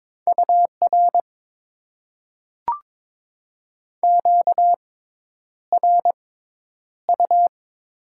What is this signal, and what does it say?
Morse code, heard as a keyed tone, about 22 wpm, spaced as for 10 wpm, 700 Hz. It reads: UR QRU
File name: Top 90 QSO Elements - 6x Character Spacing - 22wpm